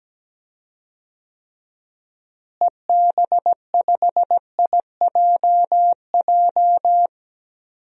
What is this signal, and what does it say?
Morse code, heard as a keyed tone, about 17 wpm, 700 Hz. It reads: EB5IJJ
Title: DXCC-100 Call Signs - 6 Characters - Repeated 2x - 17wpm